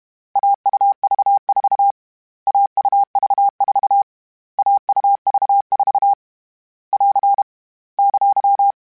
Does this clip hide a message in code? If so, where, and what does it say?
Yes: Morse code from 0.4 to 8.7 s, heard as a keyed tone, about 32 wpm, 800 Hz. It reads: AUV4 AUV4 AUV4 <AR> !